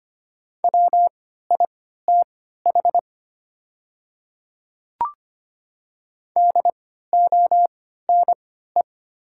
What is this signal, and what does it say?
Morse code, heard as a keyed tone, about 25 wpm, spaced as for 14 wpm, 690 Hz. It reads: WITH DONE
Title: 3x Character Spacing - Top 500 Words - 25wpm